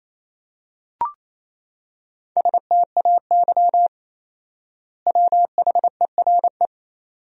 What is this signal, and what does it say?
Morse code, heard as a keyed tone, about 28 wpm, 700 Hz. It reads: STAY WHERE
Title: 4x Word-Spacing - Sentences from Top 500 Words - 28wpm